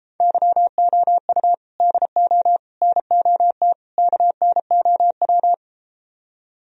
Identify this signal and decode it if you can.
Morse code, heard as a keyed tone, about 33 wpm, 700 Hz. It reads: YOU DO NOT KNOW